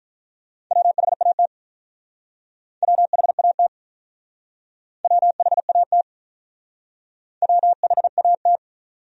Morse code, heard as a keyed tone, about 46 wpm, 700 Hz. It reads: WHAT WHAT WHAT WHAT